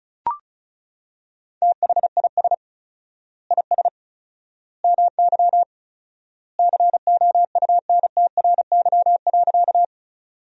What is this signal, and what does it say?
Morse code, heard as a keyed tone, about 36 wpm, 700 Hz. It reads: THIS IS MY COUNTRY.
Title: 4x Word-Spacing - Sentences from Top 300 Words - 35wpm